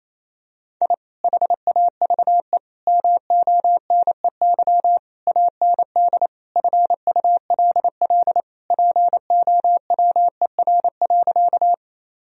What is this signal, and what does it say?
Morse code, heard as a keyed tone, about 28 wpm, 700 Hz. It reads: I HAVE MONEY AND FULL POWER.
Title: Sentences from Top 1000 Words - Review Part 3 - 28wpm